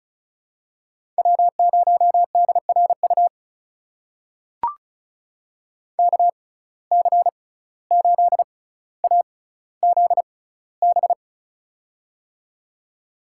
Morse code, heard as a keyed tone, about 36 wpm, spaced as for 17 wpm, 700 Hz. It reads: W0DRU KC8AZB
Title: USA Call Signs - 6x Character Spacing - 35wpm